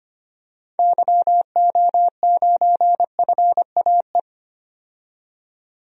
Morse code, heard as a keyed tone, about 25 wpm, 700 Hz. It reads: YO9FAE
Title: DXCC-100 Call Signs - 6 Characters - 25wpm